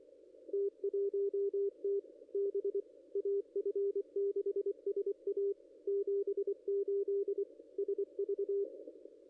Morse code, heard as a keyed tone, about 24 wpm, 410 Hz, with radio noise. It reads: T1T B AF6SA 78 SV
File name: contests / ss-cw-k5tr-op-n5rz -0200